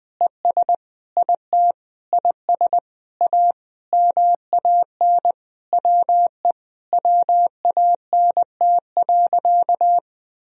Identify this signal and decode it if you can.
Morse code, heard as a keyed tone, about 20 wpm, 700 Hz. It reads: ES IT IS A MAN WE WANT.
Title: Sentences from Top 200 Words and Abbreviations 20wpm